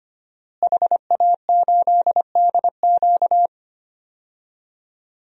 Morse code, heard as a keyed tone, about 25 wpm, 700 Hz. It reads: HA8DQ
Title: DXCC-100 Call Signs - 5 Characters - Repeated 3x - 25wpm